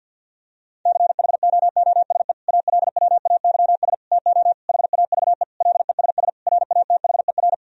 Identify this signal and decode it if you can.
Morse code, heard as a keyed tone, about 50 wpm, 700 Hz. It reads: CHOOSE ALWAYS TO HAVE LESS RATHER
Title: Sentences from Top 1000 Words - Review Part 4 - 50wpm